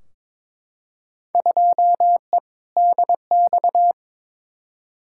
Morse code, heard as a keyed tone, about 22 wpm, 700 Hz. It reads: 2E DX
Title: Field Day Exchanges - 22wpm